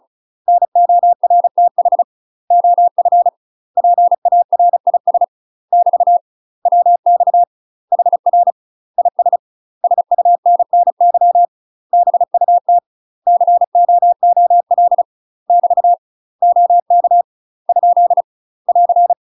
Morse code, heard as a keyed tone, about 35 wpm, 700 Hz. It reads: NORTH OF PARIS <BT> WX HR IS SUNNY BUT COOL <BT> OK ? <AR>
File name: UK MASTER QSOs TRAINING / 35WPM / UK MASTER QSOs 0002